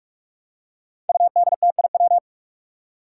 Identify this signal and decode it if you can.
Morse code, heard as a keyed tone, about 46 wpm, 700 Hz. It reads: UDTIW
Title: Five Challenging Letter-Number Combinations 45wpm